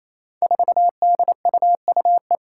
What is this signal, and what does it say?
Morse code, heard as a keyed tone, about 28 wpm, 700 Hz. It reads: 4DUUE
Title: Five Challenging Letter-Number Combinations 28wpm